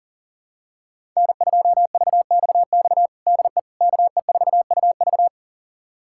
Morse code, heard as a keyed tone, about 37 wpm, 700 Hz. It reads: N2VXX DE KE4UV
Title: USA Callsign DE Callsign - 5 Characters - 40wpm